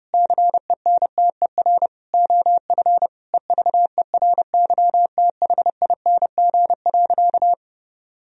Morse code, heard as a keyed tone, about 30 wpm, 700 Hz. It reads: CENTER OF EVERYTHING.
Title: Sentences from Top 1000 Words - Review Part 2 - 30wpm